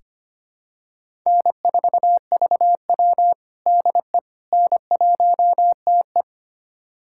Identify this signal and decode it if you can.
Morse code, heard as a keyed tone, about 25 wpm, 700 Hz. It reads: N4VW DE N1TE
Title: USA Callsign DE Callsign - 4 Characters - 25wpm